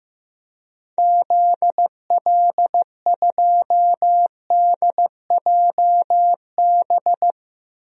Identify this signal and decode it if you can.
Morse code, heard as a keyed tone, about 15 wpm, 700 Hz. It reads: ZL2DJB
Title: DXCC-100 Call Signs - 6 Characters - 15wpm